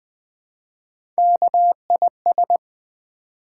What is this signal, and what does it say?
Morse code, heard as a keyed tone, about 20 wpm, 700 Hz. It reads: KIS
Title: Build A Word - 1 Letter At A Time - 4-Letter Words - 20wpm